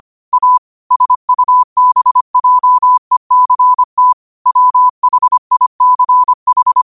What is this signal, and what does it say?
Morse code, heard as a keyed tone, about 25 wpm, 1000 Hz. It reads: A SUBJECT WHICH
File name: qotdfiles / 25wpm